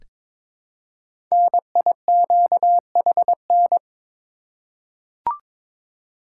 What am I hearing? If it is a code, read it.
Morse code, heard as a keyed tone, about 22 wpm, 700 Hz. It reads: NIQHN
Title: Code Groups - 5 Letter:Numbers - 22wpm